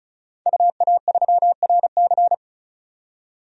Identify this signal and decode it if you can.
Morse code, heard as a keyed tone, about 34 wpm, 700 Hz. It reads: UA3RC
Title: DXCC-100 Call Signs - 5 Characters - 35wpm